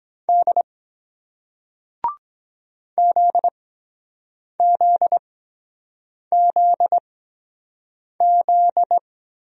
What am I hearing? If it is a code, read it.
Morse code, heard as a keyed tone, about 23 wpm, spaced as for 7 wpm, 700 Hz. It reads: D ZZZZ